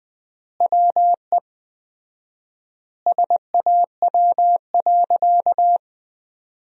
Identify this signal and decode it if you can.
Morse code, heard as a keyed tone, about 20 wpm, 700 Hz. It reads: WE SAW.
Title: 4x Word-Spacing - Sentences from Top 300 Words - 20wpm